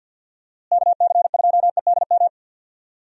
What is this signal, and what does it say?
Morse code, heard as a keyed tone, about 49 wpm, 700 Hz. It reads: KC2EDM
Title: USA Call Signs - 6 Characters - Repeated 2x - 50wpm